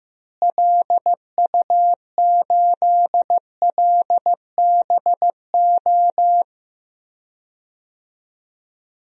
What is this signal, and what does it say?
Morse code, heard as a keyed tone, about 15 wpm, 700 Hz. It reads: LU8LBO
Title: DXCC-100 Call Signs - 6 Characters - Repeated 2x - 15wpm